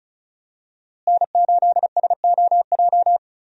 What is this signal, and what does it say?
Morse code, heard as a keyed tone, about 34 wpm, 700 Hz. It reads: N8SOJ
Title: USA Call Signs - 5 Characters - Repeated 2x - 35wpm